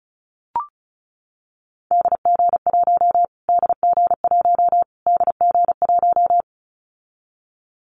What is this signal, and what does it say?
Morse code, heard as a keyed tone, about 34 wpm, 700 Hz. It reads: DG1 DG1 DG1
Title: Callsign Prefixes for Top 25 DXCC - Repeated 3x - 35wpm